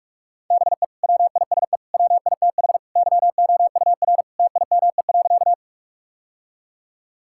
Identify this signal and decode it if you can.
Morse code, heard as a keyed tone, about 46 wpm, 700 Hz. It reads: BE WISE WITH YOUR TIME.
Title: Sentences from Top 1600 Words 45wpm